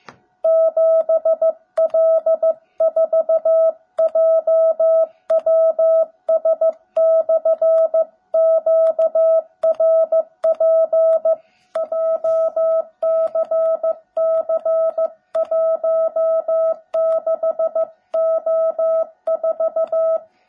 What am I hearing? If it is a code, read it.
Morse code, heard as a keyed tone, about 14 wpm, 650 Hz. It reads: 7L4JWS/QRP JCC16O4
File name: jh8cbh / 2021cw / cw17b